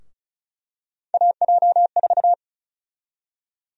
Morse code, heard as a keyed tone, about 36 wpm, 700 Hz. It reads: AJ4
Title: USA Call Sign Prefixes - Repeated 2x - 35wpm